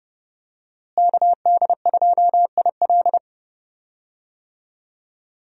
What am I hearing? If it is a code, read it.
Morse code, heard as a keyed tone, about 30 wpm, 700 Hz. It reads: KD2IL
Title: USA Call Signs - 5 Characters - Repeated 2x - 30wpm